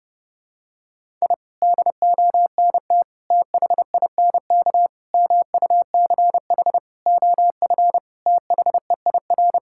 Morse code, heard as a keyed tone, about 30 wpm, 700 Hz. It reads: I DONT THINK MUCH OF THEIR
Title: Sentences from Top 1000 Words - Review Part 6 - 30wpm